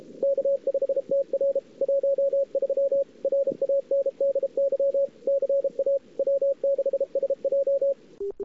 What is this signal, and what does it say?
Morse code, heard as a keyed tone, about 33 wpm, 550 Hz, with radio noise. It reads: K5TR 13 RANDY CA W6SJ
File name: CWSprint / Feb2007 / 2007-feb-cw-sprint-k5tr-op-n5rz -00:00